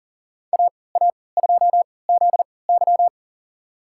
Morse code, heard as a keyed tone, about 39 wpm, spaced as for 22 wpm, 700 Hz. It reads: AA2ZY
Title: USA Call Signs - 5 Characters - 3x Character Spacing - 40wpm